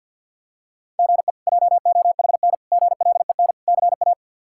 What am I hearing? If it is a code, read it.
Morse code, heard as a keyed tone, about 46 wpm, 700 Hz. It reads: GE JOHN GLEN CA
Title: K1USN SST QSO Practice - 50wpm